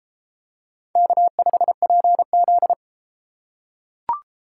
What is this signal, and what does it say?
Morse code, heard as a keyed tone, about 32 wpm, 700 Hz. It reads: K5PZ